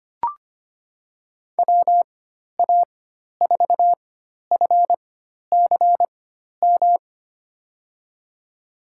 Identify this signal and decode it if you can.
Morse code, heard as a keyed tone, about 25 wpm, spaced as for 12 wpm, 700 Hz. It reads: WA4FCM
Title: USA Call Signs - 6 Characters - 4x Character Spacing - 25wpm